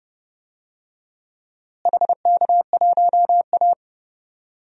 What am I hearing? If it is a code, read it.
Morse code, heard as a keyed tone, about 32 wpm, 700 Hz. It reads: HK1A
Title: DXCC-100 Call Signs - 4 Characters - Repeated 3x - 30wpm